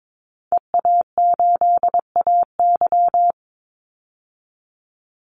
Morse code, heard as a keyed tone, about 22 wpm, 700 Hz. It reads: EA8AY
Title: Call Signs Repeated 3x - 22wpm